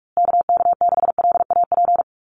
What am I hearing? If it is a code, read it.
Morse code, heard as a keyed tone, about 44 wpm, 700 Hz. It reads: KK6LAP
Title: USA Call Signs Repeated 2x 45wpm